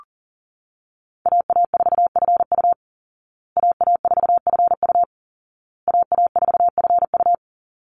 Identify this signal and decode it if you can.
Morse code, heard as a keyed tone, about 40 wpm, 700 Hz. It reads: AA4FU AA4FU AA4FU
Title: Call Signs Repeated 2x - 40wpm